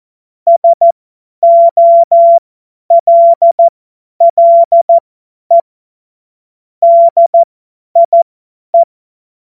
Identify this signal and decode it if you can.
Morse code, heard as a keyed tone, about 14 wpm, spaced as for 10 wpm, 680 Hz. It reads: SOLLE DIE
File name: Morsekurs / Witze / Witz 16. Mai 2023 50 F70